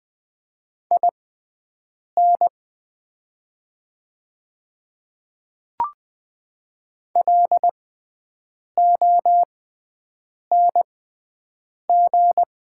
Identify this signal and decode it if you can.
Morse code, heard as a keyed tone, about 20 wpm, spaced as for 7 wpm, 690 Hz. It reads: IN LONG
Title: 6x Character Spacing - Top 100 Words - 20wpm